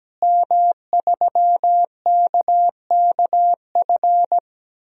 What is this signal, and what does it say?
Morse code, heard as a keyed tone, about 17 wpm, 700 Hz. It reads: M3KKF
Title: DXCC-100 Call Signs - 5 Characters - Repeated 3x - 17wpm